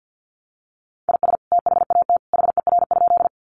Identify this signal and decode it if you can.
Morse code, heard as a keyed tone, about 50 wpm, 700 Hz. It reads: IS THAT HER?
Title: Sentences from Top 100 Words 50wpm